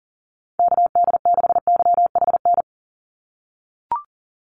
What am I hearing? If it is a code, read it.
Morse code, heard as a keyed tone, about 42 wpm, 700 Hz. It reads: KD6YHN
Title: USA Call Signs 40wpm